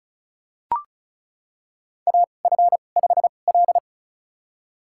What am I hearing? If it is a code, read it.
Morse code, heard as a keyed tone, about 34 wpm, spaced as for 25 wpm, 700 Hz. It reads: AF5L
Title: USA Call Signs - 4 Characters - 2x Character Spacing - 35wpm